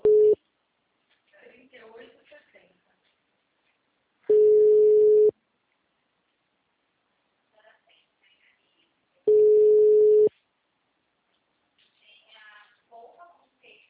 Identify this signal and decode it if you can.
Ringback tone (Europe, 425 Hz), with speech.